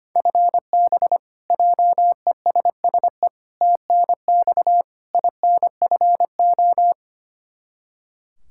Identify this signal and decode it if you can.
Morse code, heard as a keyed tone, about 25 wpm, 700 Hz. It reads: FB JESSE TNX INFO
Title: K1USN SST QSO Practice 25wpm